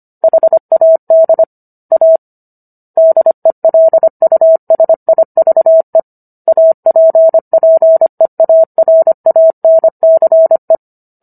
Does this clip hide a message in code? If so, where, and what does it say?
Yes: Morse code from 0.2 to 10.8 s, heard as a keyed tone, about 25 wpm, 650 Hz. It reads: HAD A DELUSIVE APPEARANCE